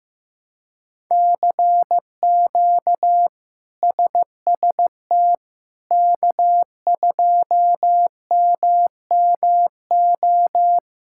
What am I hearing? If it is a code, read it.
Morse code, heard as a keyed tone, about 15 wpm, 700 Hz. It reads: CQ SST K2MMO
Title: K1USN SST QSO Practice - 15wpm